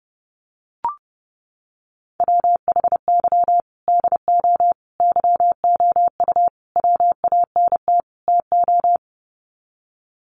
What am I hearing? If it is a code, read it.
Morse code, heard as a keyed tone, about 30 wpm, 700 Hz. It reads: WHY DO YOU WANT TO
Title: Sets of 5 Words Encore2 30wpm